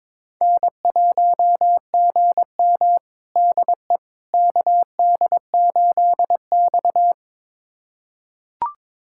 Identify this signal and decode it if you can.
Morse code, heard as a keyed tone, about 22 wpm, 700 Hz. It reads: N1GM DE KD8X
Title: USA Callsign DE Callsign - 4 Characters - 22wpm